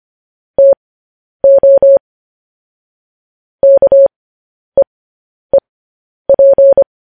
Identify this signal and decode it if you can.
Morse code, heard as a keyed tone, about 25 wpm, spaced as for 10 wpm, 550 Hz. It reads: TO KEEP
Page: news-headlines-in-morse-code-at-10-wpm